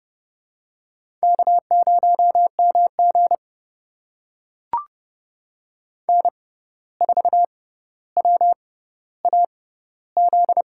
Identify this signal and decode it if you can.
Morse code, heard as a keyed tone, about 32 wpm, spaced as for 15 wpm, 700 Hz. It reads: K0MG N4WAZ
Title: USA Call Signs - 6x Character Spacing - 30wpm